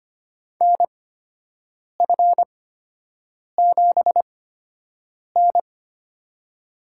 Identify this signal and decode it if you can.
Morse code, heard as a keyed tone, about 25 wpm, spaced as for 7 wpm, 700 Hz. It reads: NF7N
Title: USA Call Signs - 4 Characters - 8x Character Spacing - 25wpm